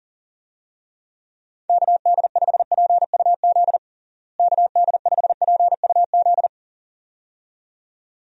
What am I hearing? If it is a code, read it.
Morse code, heard as a keyed tone, about 40 wpm, 700 Hz. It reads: KD5PUZ KD5PUZ
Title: USA Call Signs - 6 Characters - Repeated 2x - 40wpm